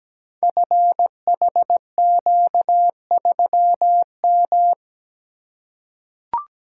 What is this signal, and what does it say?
Morse code, heard as a keyed tone, about 17 wpm, 700 Hz. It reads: FHQ3M
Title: Code Groups - 5 Letter:Numbers - 17wpm